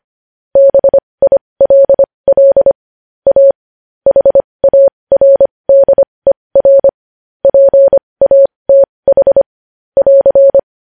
Morse code, heard as a keyed tone, about 25 wpm, spaced as for 20 wpm, 550 Hz. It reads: BILL A HARDER PATH <AR>
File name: podcast / News headlines-1x-25x20